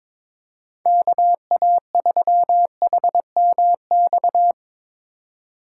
Morse code, heard as a keyed tone, about 22 wpm, 700 Hz. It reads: KA3HMX
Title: USA Call Signs - 6 Characters - 4x Character Spacing - 22wpm